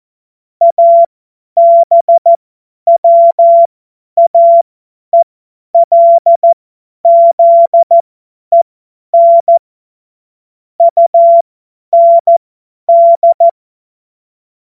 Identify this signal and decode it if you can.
Morse code, heard as a keyed tone, about 14 wpm, spaced as for 10 wpm, 680 Hz. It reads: ABWAELZEN UND
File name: Morsekurs / Witze / Witz 23. Dezember 2025 50 F70